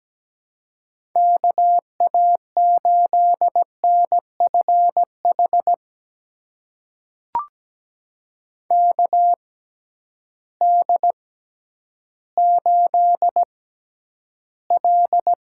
Morse code, heard as a keyed tone, about 17 wpm, spaced as for 10 wpm, 700 Hz. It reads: KA8NFH KD8L
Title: USA Call Signs - 6 Characters - 6x Character Spacing - 17wpm